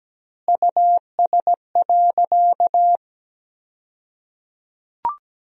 Morse code, heard as a keyed tone, about 17 wpm, 700 Hz. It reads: US.
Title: 2x Word-Spacing - Sentences from Top 500 Words - 17wpm